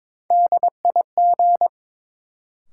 Morse code, heard as a keyed tone, about 22 wpm, 700 Hz. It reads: DIG
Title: Sentences from Top 1600 Words - New Word Drill - 22wpm